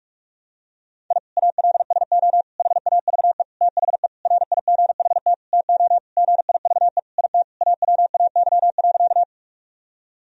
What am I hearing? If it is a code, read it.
Morse code, heard as a keyed tone, about 46 wpm, 700 Hz. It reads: I ALSO HAVE THE RIGHT TO GIVE IT AWAY.
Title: Sentences from Top 1000 Words - Review Part 1 - 45wpm